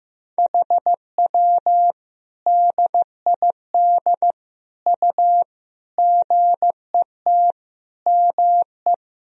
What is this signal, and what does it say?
Morse code, heard as a keyed tone, about 16 wpm, 700 Hz. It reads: HW DID U GET ME
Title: Sentences from Top 200 Words and Abbreviations 15wpm